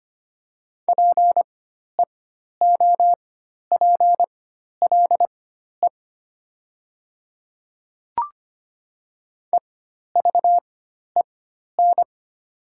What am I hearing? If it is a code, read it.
Morse code, heard as a keyed tone, about 25 wpm, spaced as for 12 wpm, 700 Hz. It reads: PEOPLE EVEN
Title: 4x Character Spacing - Top 300 Words - 25wpm